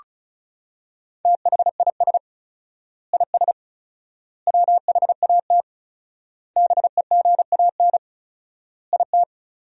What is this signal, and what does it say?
Morse code, heard as a keyed tone, about 34 wpm, 700 Hz. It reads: THIS IS WHAT BEGAN IT
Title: 4x Word-Spacing - Sentences from Top 300 Words - 35wpm